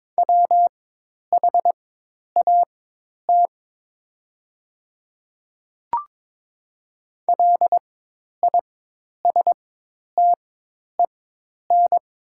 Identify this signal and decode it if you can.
Morse code, heard as a keyed tone, about 22 wpm, spaced as for 10 wpm, 700 Hz. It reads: WHAT LISTEN
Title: 4x Character Spacing - Top 500 Words - 22wpm